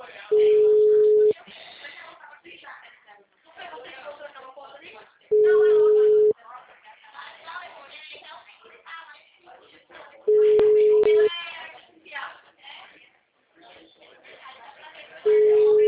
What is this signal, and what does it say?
Ringback tone (Europe, 425 Hz), with speech.
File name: out- -2012- - - .10539